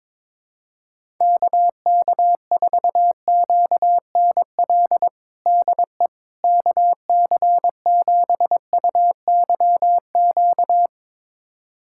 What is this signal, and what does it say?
Morse code, heard as a keyed tone, about 22 wpm, 700 Hz. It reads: KK4QNL DE KC7UYQ
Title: USA Callsign DE Callsign - 6 Characters - 22wpm